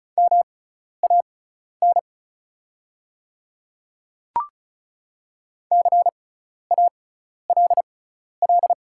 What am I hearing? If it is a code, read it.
Morse code, heard as a keyed tone, about 36 wpm, spaced as for 12 wpm, 690 Hz. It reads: MAN CALL
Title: 6x Character Spacing - Top 300 Words - 35wpm